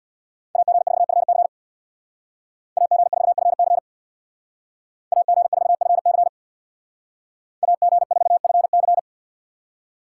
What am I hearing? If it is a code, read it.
Morse code, heard as a keyed tone, about 59 wpm, 700 Hz. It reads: AG4FC AG4FC AG4FC AG4FC
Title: Speed Racing - USA 2x2 Call Signs 50wpm